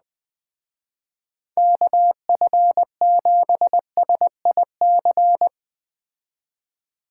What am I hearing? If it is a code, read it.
Morse code, heard as a keyed tone, about 20 wpm, 700 Hz. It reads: KF7SIC
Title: Speed Racing - USA Call Signs 17wpm